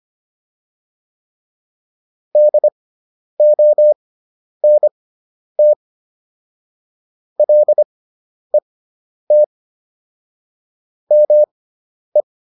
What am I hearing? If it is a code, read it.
Morse code, heard as a keyed tone, about 25 wpm, spaced as for 10 wpm, 600 Hz. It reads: DONT LET ME